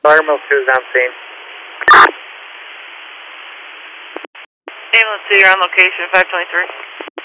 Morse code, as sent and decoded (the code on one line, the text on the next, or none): none